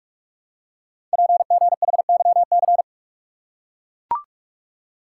Morse code, heard as a keyed tone, about 44 wpm, 700 Hz. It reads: PGHYC
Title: Code Groups - 5 Letters - 45wpm